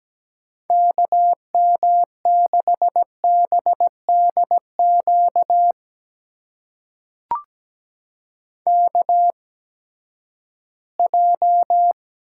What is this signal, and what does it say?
Morse code, heard as a keyed tone, about 17 wpm, spaced as for 12 wpm, 700 Hz. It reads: KM6BDQ KJ
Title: USA Call Signs - 6 Characters - 8x Character Spacing - 17wpm